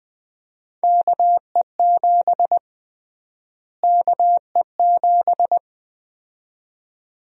Morse code, heard as a keyed tone, about 20 wpm, 700 Hz. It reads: KE7 KE7
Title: USA Call Sign Prefixes - Repeated 2x - 20wpm